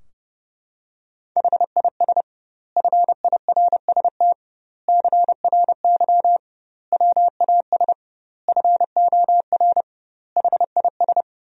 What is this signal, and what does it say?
Morse code, heard as a keyed tone, about 32 wpm, 700 Hz. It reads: HIS FIRST CRY WAS FOR HIS
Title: 2x Word-Spacing - Sentences from Top 500 Words - 30wpm